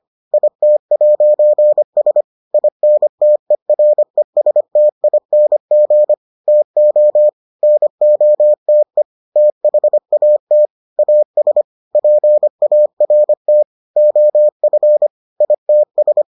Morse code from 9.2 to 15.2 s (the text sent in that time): THAT AS PART OF